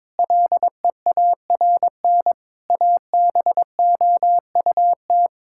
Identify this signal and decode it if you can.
Morse code, heard as a keyed tone, about 22 wpm, 700 Hz. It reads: LEARN ABOUT
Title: Sentences from Top 1000 Words - Review Part 6 - 22wpm